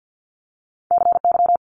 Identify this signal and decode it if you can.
Morse code, heard as a keyed tone, about 50 wpm, 700 Hz. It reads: CY
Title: Multi-letter Phonemes Quick Fire 50wpm